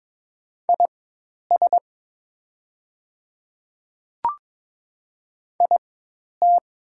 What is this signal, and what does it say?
Morse code, heard as a keyed tone, about 22 wpm, spaced as for 10 wpm, 700 Hz. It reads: IS IT
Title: 4x Character Spacing - Top 200 Words - 22wpm